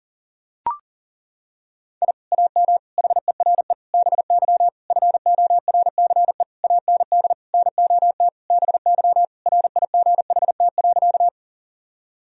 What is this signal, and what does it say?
Morse code, heard as a keyed tone, about 37 wpm, 700 Hz. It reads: I AM HERE BY FORCE AND NOT BY RIGHT.
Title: Sentences from Top 1000 Words - Review Part 3 - 40wpm